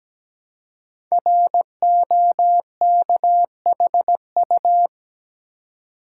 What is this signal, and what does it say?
Morse code, heard as a keyed tone, about 17 wpm, 700 Hz. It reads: ROKHU
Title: Code Groups - 5 Letters - 17wpm